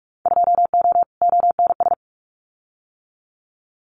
Morse code, heard as a keyed tone, about 44 wpm, 700 Hz. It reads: 2O ONS
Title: Winter Field Day Exchanges Encore 45wpm